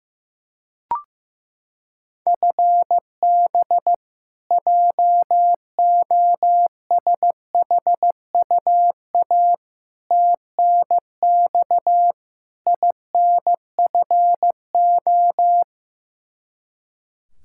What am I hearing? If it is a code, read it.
Morse code, heard as a keyed tone, about 16 wpm, 700 Hz. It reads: FB JOSHUA TNX INFO
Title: K1USN SST QSO Practice 15wpm